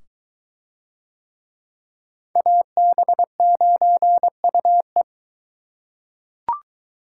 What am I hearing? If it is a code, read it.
Morse code, heard as a keyed tone, about 21 wpm, 700 Hz. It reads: AB9UE